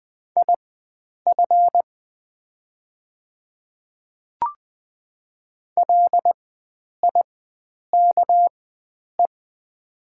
Morse code, heard as a keyed tone, about 20 wpm, spaced as for 9 wpm, 700 Hz. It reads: IF LIKE
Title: 4x Character Spacing - Top 100 Words - 20wpm